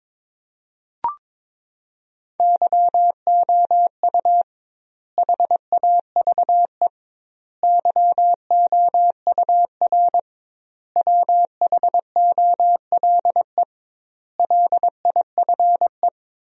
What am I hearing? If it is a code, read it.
Morse code, heard as a keyed tone, about 22 wpm, 700 Hz. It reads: YOU HAVE YOUR WHOLE LIFE
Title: 2x Word-Spacing - Sentences from Top 400 Words - 22wpm